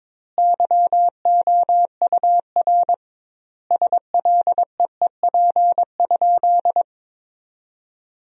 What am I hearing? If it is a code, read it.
Morse code, heard as a keyed tone, about 22 wpm, 700 Hz. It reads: YOUR SLEEP?
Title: 2x Word-Spacing - Sentences from Top 500 Words - 22wpm